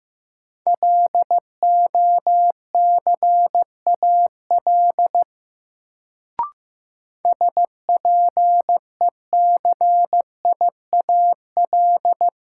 Morse code, heard as a keyed tone, about 15 wpm, 700 Hz. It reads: LOCAL SPECIAL
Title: Sentences from Top 1000 Words - Review Part 3 - New Word Drill - 15wpm